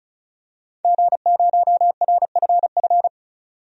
Morse code, heard as a keyed tone, about 34 wpm, 700 Hz. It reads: G0RFF
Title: DXCC-100 Call Signs - 5 Characters - 35wpm